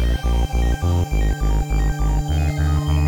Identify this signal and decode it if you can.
There is no code.